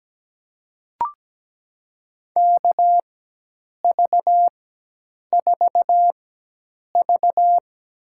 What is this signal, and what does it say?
Morse code, heard as a keyed tone, about 17 wpm, spaced as for 8 wpm, 700 Hz. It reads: KV4V
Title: USA Call Signs - 4x Character Spacing - 17wpm